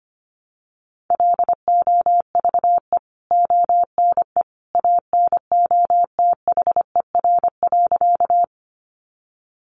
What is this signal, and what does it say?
Morse code, heard as a keyed tone, about 25 wpm, 700 Hz. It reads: LOVE ONE ANOTHER.
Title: Sentences from Top 600 Words 25wpm